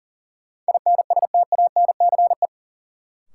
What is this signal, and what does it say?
Morse code, heard as a keyed tone, about 40 wpm, 700 Hz. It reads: INSTANCE
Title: Sentences from Top 800 Words - New Word Drill - 40wpm